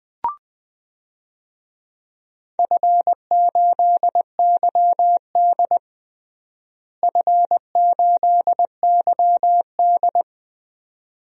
Morse code, heard as a keyed tone, about 20 wpm, 700 Hz. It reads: F8YD F8YD
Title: DXCC-100 Call Signs - 4 Characters - Repeated 2x - 20wpm